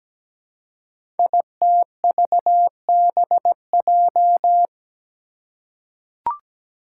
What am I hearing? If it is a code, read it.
Morse code, heard as a keyed tone, about 17 wpm, 700 Hz. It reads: ITVBJ
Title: Code Groups - 5 Letters - 17wpm